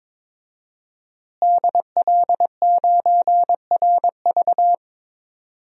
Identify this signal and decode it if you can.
Morse code, heard as a keyed tone, about 22 wpm, 700 Hz. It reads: DL9RV
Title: DXCC-100 Call Signs - 5 Characters - Repeated 2x - 22wpm